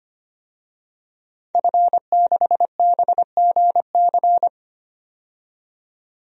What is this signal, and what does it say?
Morse code, heard as a keyed tone, about 25 wpm, 700 Hz. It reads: F6BGC
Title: DXCC-100 Call Signs - 5 Characters - 25wpm